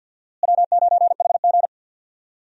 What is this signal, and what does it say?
Morse code, heard as a keyed tone, about 52 wpm, 700 Hz. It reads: W9HG
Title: USA Call Signs - 4 Characters - Repeated 3x - 50wpm